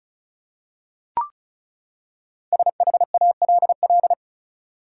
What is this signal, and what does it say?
Morse code, heard as a keyed tone, about 36 wpm, 700 Hz. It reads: SHALL
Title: Five Letter Words - 35wpm